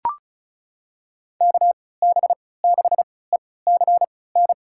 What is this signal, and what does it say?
Morse code, heard as a keyed tone, about 36 wpm, spaced as for 20 wpm, 700 Hz. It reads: KB6ECN
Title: USA Call Signs - 6 Characters - 3x Character Spacing - 35wpm